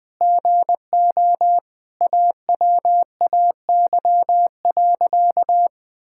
Morse code, heard as a keyed tone, about 21 wpm, 700 Hz. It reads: GO AWAY.
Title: Sentences with 4-Word Phrases - 20wpm